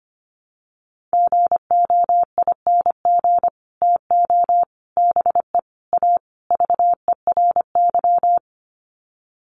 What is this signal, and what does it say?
Morse code, heard as a keyed tone, about 25 wpm, 700 Hz. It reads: GOING TO BE A VERY
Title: Sets of 5 Words Encore 25wpm